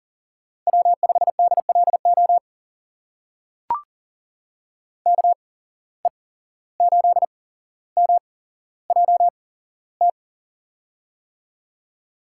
Morse code, heard as a keyed tone, about 40 wpm, spaced as for 16 wpm, 700 Hz. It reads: W5DLO KE8MJT
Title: USA Call Signs - 8x Character Spacing - 40wpm